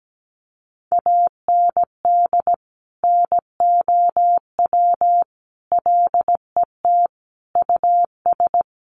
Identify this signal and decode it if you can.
Morse code, heard as a keyed tone, about 17 wpm, 700 Hz. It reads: AND NOW LET US
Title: Sentences from Top 1500 Words 17wpm